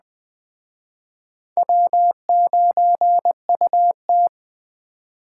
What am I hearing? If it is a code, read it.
Morse code, heard as a keyed tone, about 21 wpm, 700 Hz. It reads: W9UT
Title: Speed Racing - USA Call Signs 17wpm